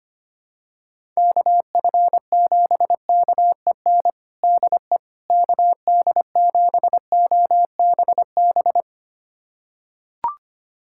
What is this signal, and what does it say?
Morse code, heard as a keyed tone, about 25 wpm, 700 Hz. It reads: KF7KEN DE KD7OBB
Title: USA Callsign DE Callsign - 6 Characters - 25wpm